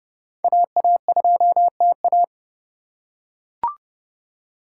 Morse code, heard as a keyed tone, about 30 wpm, 700 Hz. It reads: AA2TA
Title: USA Call Signs - 5 Characters - 3x Character Spacing - 30wpm